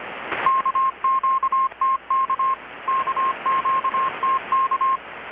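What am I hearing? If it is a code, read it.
Morse code, heard as a keyed tone, about 25 wpm, 1060 Hz. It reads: KQTK KQTK
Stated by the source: Morse from KPH's Night of Nights XX maritime radiotelegraphy event, received in Chiapas, Mexico about 3,800 km away on a Kenwood R-600 with a grounded 10 m long-wire antenna.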